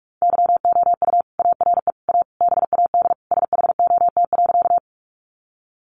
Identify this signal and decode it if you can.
Morse code, heard as a keyed tone, about 44 wpm, 700 Hz. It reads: YOU ARE A BAD SHOT.
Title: Sentences from Top 1100 Words 45wpm